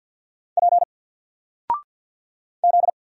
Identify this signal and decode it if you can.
Morse code, heard as a keyed tone, about 48 wpm, 700 Hz. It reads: P Z
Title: Single Character (Letter:Number:Punctuation:Prosign) Alt 50wpm